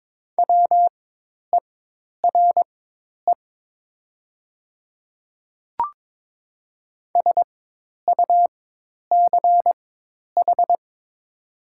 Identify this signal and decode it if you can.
Morse code, heard as a keyed tone, about 22 wpm, spaced as for 10 wpm, 700 Hz. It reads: WERE SUCH
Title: 4x Character Spacing - Top 200 Words - 22wpm